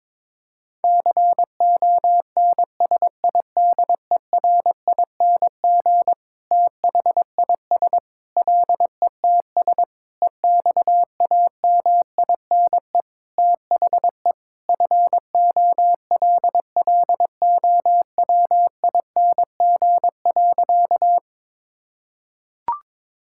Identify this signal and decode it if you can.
Morse code, heard as a keyed tone, about 22 wpm, 700 Hz. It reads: CONSIDERING THIS LETS EXAMINE THE FOLLOWING.
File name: Sentences with 9-Word Phrases - 22wpm